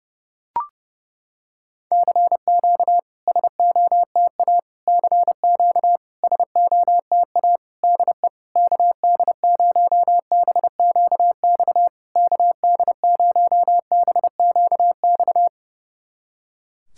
Morse code, heard as a keyed tone, about 30 wpm, 700 Hz. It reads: CQ SOTA CQ SOTA DE KD0BQX KD0BQX
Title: SOTA QSO Practice 30wpm